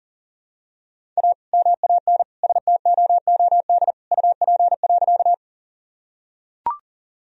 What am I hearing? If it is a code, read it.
Morse code, heard as a keyed tone, about 37 wpm, 700 Hz. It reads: A MAN STOOD UP.
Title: Sentences from Top 1000 Words - Review Part 2 - 40wpm